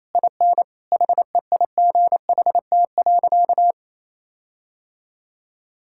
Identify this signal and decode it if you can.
Morse code, heard as a keyed tone, about 28 wpm, 700 Hz. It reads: IN HEIGHT.
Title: Sentences from Top 1600 Words 28wpm